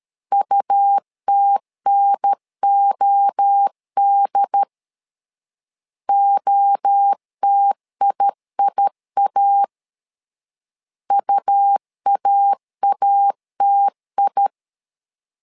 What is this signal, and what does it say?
Morse code, heard as a keyed tone, about 13 wpm, 800 Hz. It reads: UTNOD OTIIA UAATI
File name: LEZIONI / 3 MWRO / 3 MWRO 12wpm 1730